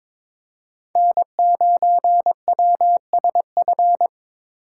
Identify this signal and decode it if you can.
Morse code, heard as a keyed tone, about 22 wpm, 700 Hz. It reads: N9WSF
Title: USA Call Signs - 6x Character Spacing - 22wpm